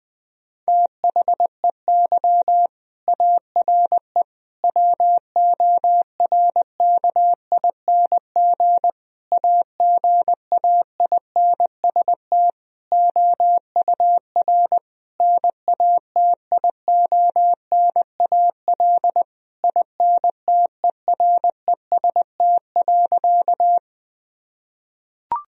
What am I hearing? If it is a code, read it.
Morse code, heard as a keyed tone, about 20 wpm, 700 Hz. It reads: THEY ARE WORKING AGAINST OUR NATIONAL INTEREST.
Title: Sentences from Top 1000 Words - Review Part 4 - 20wpm